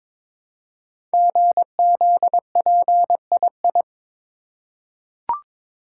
Morse code, heard as a keyed tone, about 23 wpm, 700 Hz. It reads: GZPII
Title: Code Groups - 5 Letter:Numbers - 22wpm